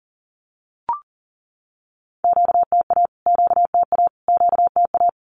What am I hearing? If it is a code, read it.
Morse code, heard as a keyed tone, about 40 wpm, 700 Hz. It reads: QTA QTA QTA
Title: Q-Codes Repeated 3x 40wpm